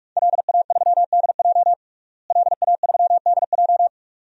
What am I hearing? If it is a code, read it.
Morse code, heard as a keyed tone, about 41 wpm, 700 Hz. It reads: LA3DJ LA3DJ
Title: DXCC-100 Call Signs - 5 Characters - Repeated 2x - 45wpm